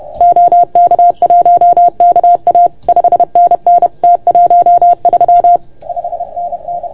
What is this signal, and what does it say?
Morse code, heard as a keyed tone, about 30 wpm, 690 Hz. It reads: OK1KA 5NN T13 5NN
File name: Archiv / CUC381 / OK1FGD CUC 381